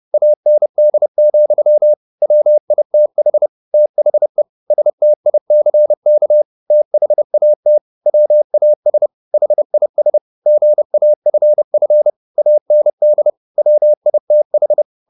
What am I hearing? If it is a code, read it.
Morse code, heard as a keyed tone, about 30 wpm, 600 Hz. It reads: AND, WITH THE STICK THAT WAS HIS GAFF AND WITH